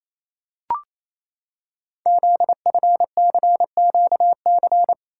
Morse code, heard as a keyed tone, about 29 wpm, 700 Hz. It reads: ZFCQC
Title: Code Groups - 5 Letters - 28wpm